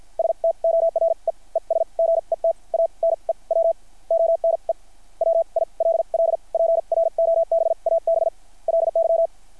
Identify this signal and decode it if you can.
Morse code, heard as a keyed tone, about 50 wpm, spaced as for 38 wpm, 660 Hz. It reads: STORE ES GET A NEW ONE WILL PROBABLY
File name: archived / CRSnet